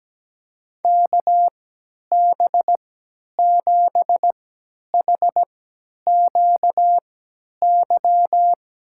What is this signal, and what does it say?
Morse code, heard as a keyed tone, about 17 wpm, spaced as for 10 wpm, 700 Hz. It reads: KB7HQY
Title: USA Call Signs - 6 Characters - 3x Character Spacing - 17wpm